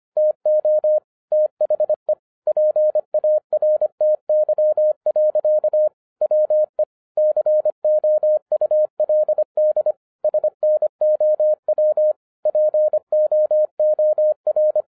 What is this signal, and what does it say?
Morse code, heard as a keyed tone, about 25 wpm, 610 Hz. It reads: TO THE PARTY. WE COULD SNOW POOR
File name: 2012-01-mp3 / n14-FUN WITH NUMBERS-25wpm